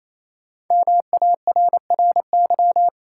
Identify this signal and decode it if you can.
Morse code, heard as a keyed tone, about 28 wpm, 700 Hz. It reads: MARRY
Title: Sentences from Top 1200 Words - New Word Drill - 28wpm